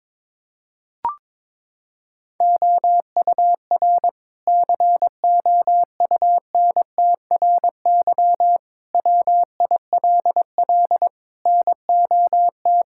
Morse code, heard as a keyed tone, about 22 wpm, 700 Hz. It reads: OUR COUNTRY WILL NOT
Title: Sentences from Top 1000 Words - Review Part 6 - 22wpm